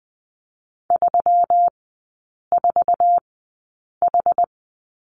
Morse code, heard as a keyed tone, about 20 wpm, spaced as for 8 wpm, 700 Hz. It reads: 34H
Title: 178 Lesson 29e - Slash - Three Characters 20wpm